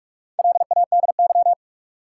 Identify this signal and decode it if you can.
Morse code, heard as a keyed tone, about 44 wpm, 700 Hz. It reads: LADY
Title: Four Letter Words - 45wpm